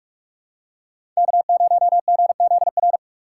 Morse code, heard as a keyed tone, about 44 wpm, 690 Hz. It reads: K0GZR
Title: USA Call Signs - 5 Characters - 2x Character Spacing - 45wpm